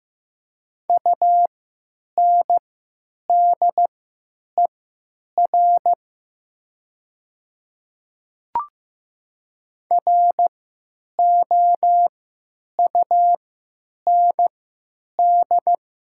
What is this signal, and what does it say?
Morse code, heard as a keyed tone, about 15 wpm, spaced as for 9 wpm, 700 Hz. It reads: UNDER ROUND
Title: 3x Character Spacing - Top 200 Words - 15wpm